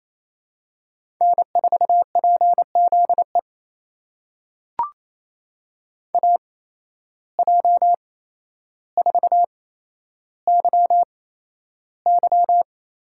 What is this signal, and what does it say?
Morse code, heard as a keyed tone, about 28 wpm, spaced as for 12 wpm, 700 Hz. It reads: N4PZE AJ4YY
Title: USA Call Signs - 8x Character Spacing - 28wpm